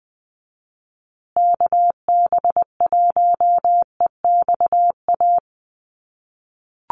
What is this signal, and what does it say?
Morse code, heard as a keyed tone, about 20 wpm, 700 Hz. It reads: KB1EXA
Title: USA Call Signs Repeated 3x 20wpm